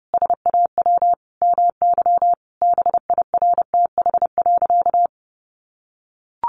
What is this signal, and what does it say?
Morse code, heard as a keyed tone, about 30 wpm, 700 Hz. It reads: SAW MY BIRTH.
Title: Sentences from Top 800 Words 30wpm